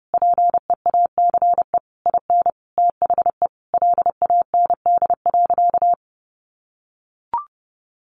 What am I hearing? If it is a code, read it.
Morse code, heard as a keyed tone, about 30 wpm, 700 Hz. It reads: PEACE IN THE LAND.
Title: Sentences from Top 1500 Words 30wpm